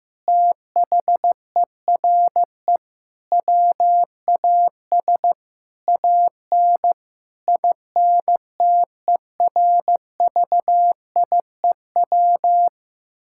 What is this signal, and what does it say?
Morse code, heard as a keyed tone, about 15 wpm, 700 Hz. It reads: THERE WAS AN INTERVIEW
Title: Sentences from Top 1600 Words 15wpm